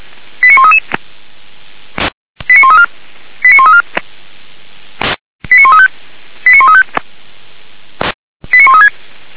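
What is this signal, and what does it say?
Five-tone selective call (ZVEI), heard as quick runs of five tones, tones of about 70 ms. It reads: 89133 89134 89134 89135 89135 89136
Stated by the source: ZVEI five-tone selective call 89132 received on an SDR by the Orlamünde fire brigade (feuerwehr-orlamuende).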